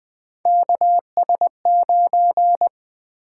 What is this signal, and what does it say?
Morse code, heard as a keyed tone, about 20 wpm, 700 Hz. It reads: KS9
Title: Three Letter-Number Combinations - 20wpm